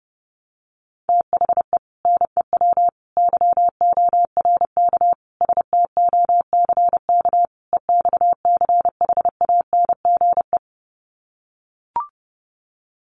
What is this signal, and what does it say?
Morse code, heard as a keyed tone, about 30 wpm, 700 Hz. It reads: THE NEW YORK STOCK EXCHANGE
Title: Sets of 5 Words 30wpm